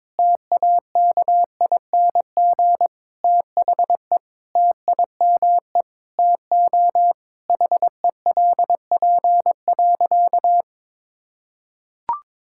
Morse code, heard as a keyed tone, about 22 wpm, 700 Hz. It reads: TAKING THE TIME TO HELP.
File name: Sentences with 7-Word Phrases - 22wpm